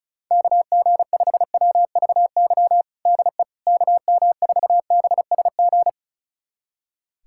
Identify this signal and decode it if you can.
Morse code, heard as a keyed tone, about 34 wpm, 700 Hz. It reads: KG5WVY DE KM4BSG
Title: USA Callsign DE Callsign - 6 Characters - 35wpm